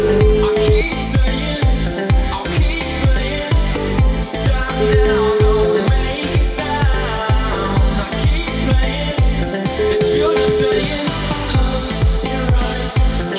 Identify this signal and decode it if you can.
Ringback tone (Europe, 425 Hz), with noise.